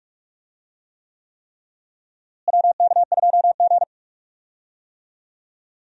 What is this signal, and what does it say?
Morse code, heard as a keyed tone, about 41 wpm, 700 Hz. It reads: WK2G
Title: USA Call Signs - 4 Characters - Repeated 3x - 45wpm